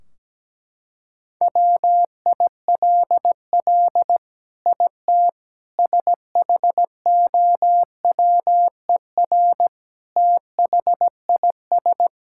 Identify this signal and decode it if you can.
Morse code, heard as a keyed tone, about 17 wpm, 700 Hz. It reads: WILL IT SHOWER THIS
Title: Sentences from Top 1600 Words 17wpm